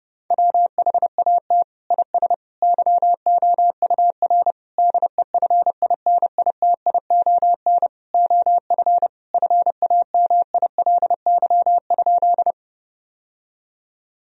Morse code, heard as a keyed tone, about 30 wpm, 700 Hz. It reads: WHAT IS YOUR DEFINITION OF FAMILY?
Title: Sentences from Top 1000 Words - Review Part 4 - 30wpm